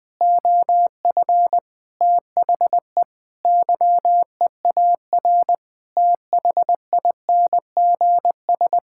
Morse code, heard as a keyed tone, about 20 wpm, 700 Hz. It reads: OF THE YEAR THINGS
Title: Sentences with 5-Word Phrases - 20wpm